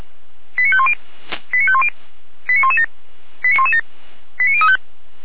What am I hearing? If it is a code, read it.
Five-tone selective call (ZVEI), heard as quick runs of five tones, tones of about 70 ms. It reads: 89510 89510 89117 89117 89035
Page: /455/SDR/ZVEI/89110/2023.01.22/